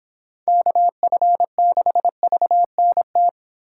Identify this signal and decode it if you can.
Morse code, heard as a keyed tone, about 25 wpm, 700 Hz. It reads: KF6VNT